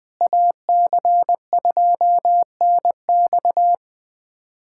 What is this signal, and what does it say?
Morse code, heard as a keyed tone, about 20 wpm, 700 Hz. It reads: AC2NX
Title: USA Callsign DE Callsign - 5 Characters - 20wpm